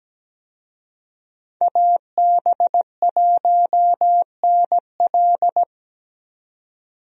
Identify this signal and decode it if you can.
Morse code, heard as a keyed tone, about 17 wpm, 700 Hz. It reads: AB1NL
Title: USA Call Signs - 5 Characters - 17wpm